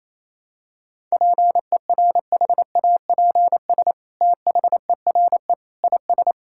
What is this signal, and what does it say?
Morse code, heard as a keyed tone, about 28 wpm, 700 Hz. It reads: PERHAPS THERE IS
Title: Sentences from Top 1000 Words - Review Part 5 - 28wpm